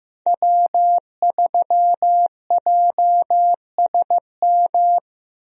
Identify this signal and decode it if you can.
Morse code, heard as a keyed tone, about 15 wpm, 700 Hz. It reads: W3JSM
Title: USA Call Signs - 5 Characters - Repeated 3x - 15wpm